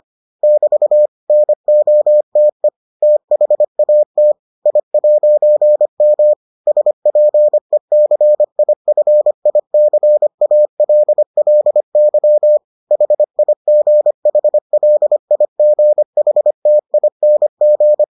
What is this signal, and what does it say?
Morse code, heard as a keyed tone, about 25 wpm, 600 Hz. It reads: <BT> NOTE THAT I'M SPECIFICALLY HIGHLIGHTING